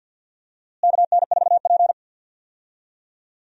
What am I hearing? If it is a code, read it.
Morse code, heard as a keyed tone, about 51 wpm, 700 Hz. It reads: KN4P
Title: USA Call Signs - 4 Characters - Repeated 3x - 50wpm